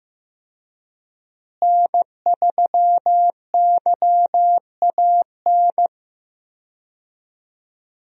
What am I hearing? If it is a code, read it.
Morse code, heard as a keyed tone, about 15 wpm, 700 Hz. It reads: N3YAN
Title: USA Call Signs - 5 Characters - Repeated 3x - 15wpm